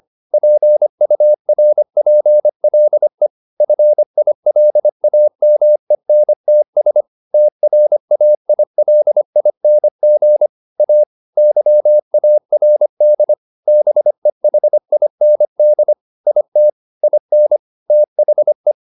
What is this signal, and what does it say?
Morse code, heard as a keyed tone, about 25 wpm, 600 Hz. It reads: PURPLE FILAMENTS TRAILING A YARD BEHIND IT IN THE
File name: the old man and the sea part 1 25wpm